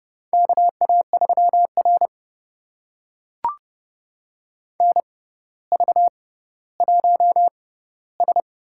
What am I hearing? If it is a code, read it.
Morse code, heard as a keyed tone, about 31 wpm, spaced as for 15 wpm, 700 Hz. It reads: KA3R NV1S
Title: USA Call Signs - 4 Characters - 6x Character Spacing - 30wpm